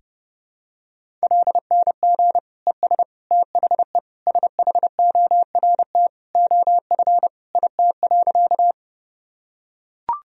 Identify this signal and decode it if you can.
Morse code, heard as a keyed tone, about 30 wpm, 700 Hz. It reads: LNG ES THE SHORT OF IT.
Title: Sentences from Top 400 Words and Abbreviations 30wpm